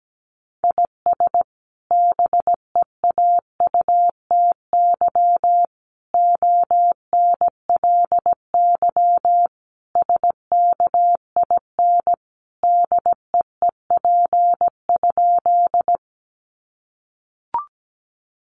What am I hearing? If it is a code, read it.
Morse code, heard as a keyed tone, about 17 wpm, 700 Hz. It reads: IS BEAUTY ONLY SKIN DEEP?
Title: Sentences from Top 1300 Words 17wpm